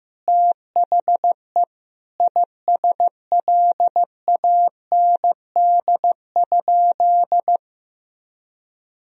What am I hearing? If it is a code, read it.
Morse code, heard as a keyed tone, about 15 wpm, 700 Hz. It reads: THE ISLAND?
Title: Sentences from Top 1000 Words - Review Part 3 - 15wpm